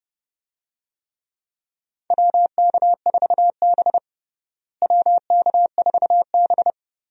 Morse code, heard as a keyed tone, about 30 wpm, 700 Hz. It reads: WK4B WK4B
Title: USA Call Signs - 4 Characters - Repeated 3x - 30wpm